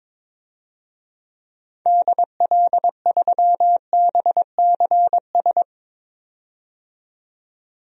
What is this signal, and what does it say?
Morse code, heard as a keyed tone, about 22 wpm, 700 Hz. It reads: DL3BCS
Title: DXCC-100 Call Signs - 6 Characters - 22wpm